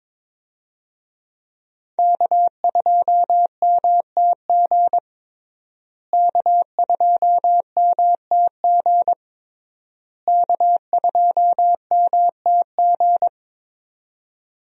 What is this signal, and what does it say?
Morse code, heard as a keyed tone, about 22 wpm, 700 Hz. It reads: K2MTG K2MTG K2MTG
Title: USA Call Signs - 5 Characters - Repeated 3x - 22wpm